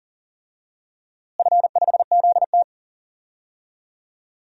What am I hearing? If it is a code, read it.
Morse code, heard as a keyed tone, about 40 wpm, 700 Hz. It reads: F5ZT
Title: DXCC-100 Call Signs - 4 Characters - Repeated 3x - 40wpm